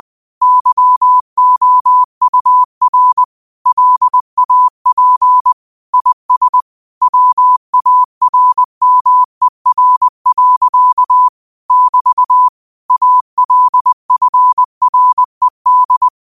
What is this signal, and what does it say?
Morse code, heard as a keyed tone, about 20 wpm, 1000 Hz. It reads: YOUR LAP IS WARMER. <BT> ALFRED